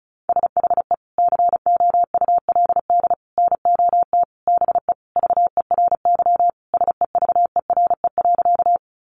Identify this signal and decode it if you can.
Morse code, heard as a keyed tone, about 34 wpm, 700 Hz. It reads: SHE COULD NOT BE VERY SEVERE.
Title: Sentences from Top 1400 Words 35wpm